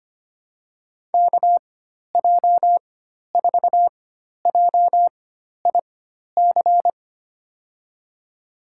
Morse code, heard as a keyed tone, about 25 wpm, spaced as for 12 wpm, 700 Hz. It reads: KJ4JIC
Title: USA Call Signs - 6 Characters - 4x Character Spacing - 25wpm